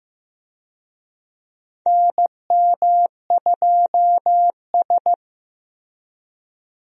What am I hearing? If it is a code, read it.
Morse code, heard as a keyed tone, about 15 wpm, 700 Hz. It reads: NM2S
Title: USA Call Signs - 4 Characters - Repeated 3x - 15wpm